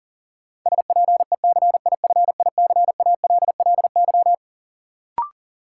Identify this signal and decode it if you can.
Morse code, heard as a keyed tone, about 40 wpm, 700 Hz. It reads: SPECIFICALLY
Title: Sentences from Top 1100 Words - New Word Drill - 40wpm